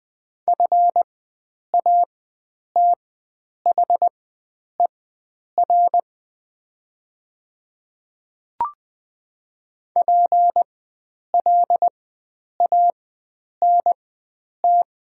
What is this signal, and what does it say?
Morse code, heard as a keyed tone, about 21 wpm, spaced as for 9 wpm, 700 Hz. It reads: FATHER PLANT
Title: 4x Character Spacing - Top 300 Words - 20wpm